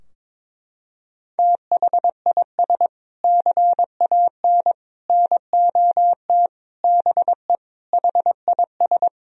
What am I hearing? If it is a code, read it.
Morse code, heard as a keyed tone, about 22 wpm, 700 Hz. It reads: THIS CAN NOT BE HIS
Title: Sentences from Top 1000 Words - Review Part 3 - 22wpm